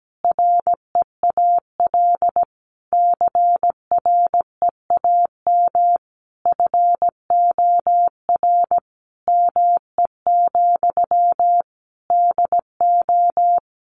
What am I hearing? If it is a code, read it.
Morse code, heard as a keyed tone, about 17 wpm, 700 Hz. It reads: REAL CREAM FOR ME, DO
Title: Sentences from Top 1500 Words 17wpm